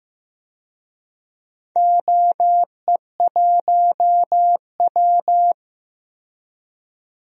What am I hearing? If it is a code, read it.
Morse code, heard as a keyed tone, about 15 wpm, 700 Hz. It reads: OE1W
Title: DXCC-100 Call Signs - 4 Characters - 15wpm